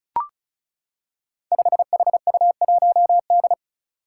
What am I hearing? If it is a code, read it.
Morse code, heard as a keyed tone, about 34 wpm, 700 Hz. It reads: 5HU1D
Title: Five Challenging Letter-Number Combinations 35wpm